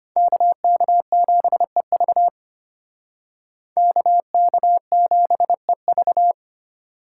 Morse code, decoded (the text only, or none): KK7EV KK7EV